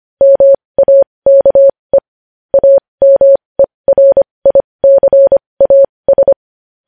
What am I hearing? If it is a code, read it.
Morse code, heard as a keyed tone, about 25 wpm, spaced as for 20 wpm, 550 Hz. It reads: MAKE AMERICAS
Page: News Headlines in Morse Code at 20 WPM